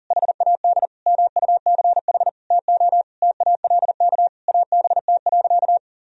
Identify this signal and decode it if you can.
Morse code, heard as a keyed tone, about 40 wpm, 700 Hz. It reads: HAD MUCH TO TALK ABT.
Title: Sentences from Top 400 Words and Abbreviations 40wpm